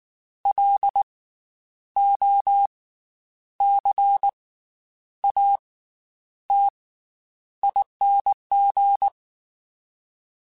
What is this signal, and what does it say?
Morse code, heard as a keyed tone, about 19 wpm, spaced as for 9 wpm, 800 Hz. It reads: LOCATING
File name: Special INGs / Special INGs-mp3s / Slow ing @18